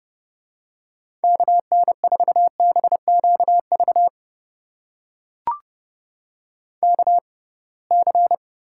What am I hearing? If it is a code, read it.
Morse code, heard as a keyed tone, about 31 wpm, spaced as for 23 wpm, 700 Hz. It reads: KN4BQV KC
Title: USA Call Signs - 6 Characters - 6x Character Spacing - 30wpm